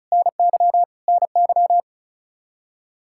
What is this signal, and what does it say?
Morse code, heard as a keyed tone, about 36 wpm, 700 Hz. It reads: NY NY
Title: Practice QSO - Exchange 1 - Beginning - 35wpm